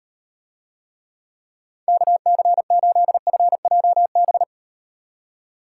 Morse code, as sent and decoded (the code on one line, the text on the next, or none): -.- -.-. ---.. ..-. .--- -...
KC8FJB